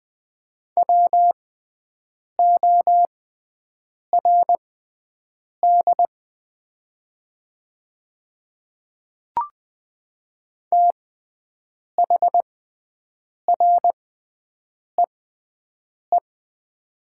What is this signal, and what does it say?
Morse code, heard as a keyed tone, about 20 wpm, spaced as for 7 wpm, 700 Hz. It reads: WORD THREE
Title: 6x Character Spacing - Top 500 Words - 20wpm